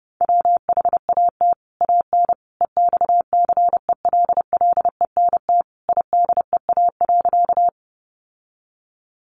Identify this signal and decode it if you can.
Morse code, heard as a keyed tone, about 30 wpm, 700 Hz. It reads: WHAT AN EXCELLENT IDEA.
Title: Sentences from Top 1000 Words 30wpm